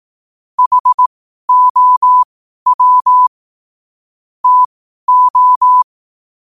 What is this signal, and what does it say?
Morse code, heard as a keyed tone, about 18 wpm, spaced as for 12 wpm, 1000 Hz. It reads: HOW TO
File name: qotdfiles / 12wpm